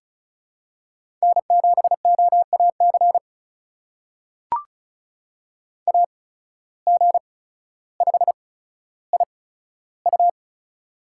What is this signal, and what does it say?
Morse code, heard as a keyed tone, about 36 wpm, spaced as for 15 wpm, 700 Hz. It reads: N7OAC AG5IU
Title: USA Call Signs - 8x Character Spacing - 35wpm